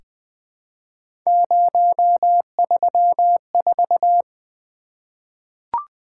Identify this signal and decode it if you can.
Morse code, heard as a keyed tone, about 21 wpm, 700 Hz. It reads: 034
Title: Three Letter-Number Combinations - 20wpm